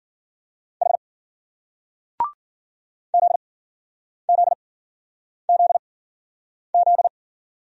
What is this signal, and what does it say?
Morse code, heard as a keyed tone, about 50 wpm, spaced as for 9 wpm, 700 Hz. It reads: H ZZZZ
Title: Speed-Racing Single Letters 40wpm